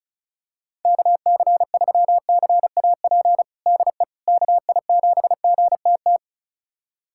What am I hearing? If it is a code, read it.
Morse code, heard as a keyed tone, about 34 wpm, 700 Hz. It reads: KC3CAP DE KI7GTT
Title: USA Callsign DE Callsign 35wpm